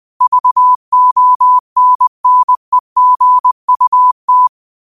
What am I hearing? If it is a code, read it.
Morse code, heard as a keyed tone, about 20 wpm, 1000 Hz. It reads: VONNEGUT
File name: qotdfiles / 20wpm